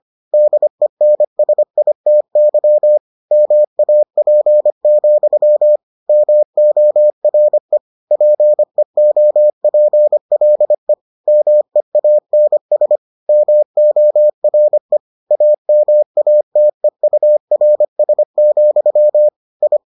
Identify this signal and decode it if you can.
Morse code, heard as a keyed tone, about 25 wpm, 600 Hz. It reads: DENSITY MAP, MORE PEOPLE MEANS MORE AMATEURS, I